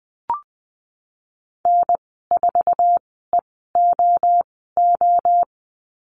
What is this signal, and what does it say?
Morse code, heard as a keyed tone, about 20 wpm, spaced as for 14 wpm, 700 Hz. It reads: N4EOO
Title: 048 Lesson 9g - Number 4 - Call Signs 20wpm